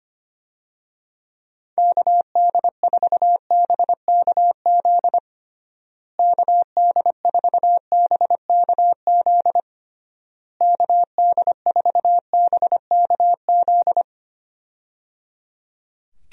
Morse code, heard as a keyed tone, about 25 wpm, 700 Hz. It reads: KD4BKZ KD4BKZ KD4BKZ
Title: USA Call Signs - 6 Characters - Repeated 3x - 25wpm